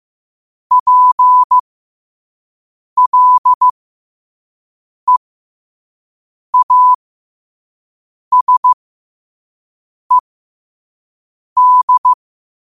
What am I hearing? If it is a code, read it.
Morse code, heard as a keyed tone, about 15 wpm, spaced as for 5 wpm, 1000 Hz. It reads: PLEASED
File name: qotdfiles / 5wpm